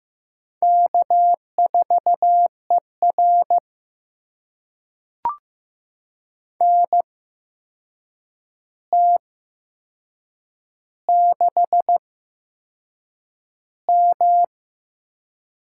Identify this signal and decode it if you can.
Morse code, heard as a keyed tone, about 15 wpm, spaced as for 6 wpm, 700 Hz. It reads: K4ER NT6M
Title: USA Call Signs - 4 Characters - 8x Character Spacing - 15wpm